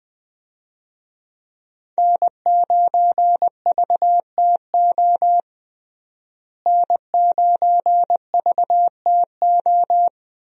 Morse code, heard as a keyed tone, about 20 wpm, 700 Hz. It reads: N9VTO N9VTO
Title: USA Call Signs - 5 Characters - Repeated 3x - 20wpm